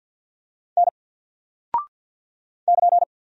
Morse code, heard as a keyed tone, about 48 wpm, 700 Hz. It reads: N <KN>
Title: Single Character (Letter:Number:Punctuation:Prosign) 50wpm